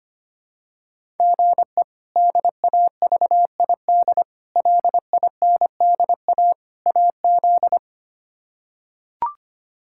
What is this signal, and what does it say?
Morse code, heard as a keyed tone, about 25 wpm, 700 Hz. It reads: GE DAVID LINDA AZ
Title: K1USN SST QSO Practice - 25wpm